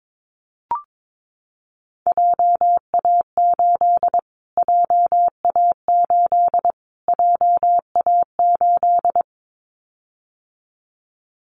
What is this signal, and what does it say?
Morse code, heard as a keyed tone, about 22 wpm, 700 Hz. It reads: JA8 JA8 JA8
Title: Callsign Prefixes for Top 25 DXCC - Repeated 3x - 22wpm